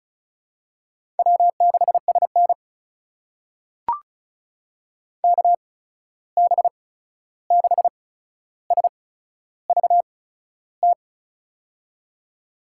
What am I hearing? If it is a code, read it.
Morse code, heard as a keyed tone, about 36 wpm, spaced as for 13 wpm, 700 Hz. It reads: W6SN KB6SVT
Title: USA Call Signs - 8x Character Spacing - 35wpm